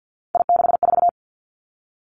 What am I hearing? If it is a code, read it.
Morse code, heard as a keyed tone, about 50 wpm, 700 Hz. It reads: I64
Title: Three Challenging Letter-Number Combinations 50wpm